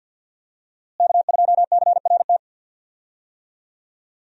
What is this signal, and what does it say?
Morse code, heard as a keyed tone, about 51 wpm, 700 Hz. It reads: K2CRT
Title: USA Call Signs - 5 Characters - Repeated 3x - 50wpm